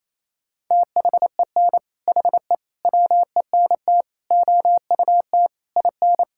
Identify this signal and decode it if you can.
Morse code, heard as a keyed tone, about 27 wpm, 700 Hz. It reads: THEN HE WENT OUT IN
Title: Sentences from Top 1000 Words - Review Part 5 - 28wpm